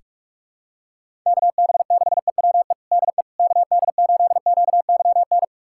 Morse code, heard as a keyed tone, about 47 wpm, 700 Hz. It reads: KB6EWE DE KD8QYN
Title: USA Callsign DE Callsign - 6 Characters - 45wpm